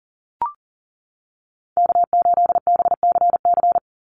Morse code, heard as a keyed tone, about 40 wpm, 700 Hz. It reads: K8BCC
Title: USA Call Signs 40wpm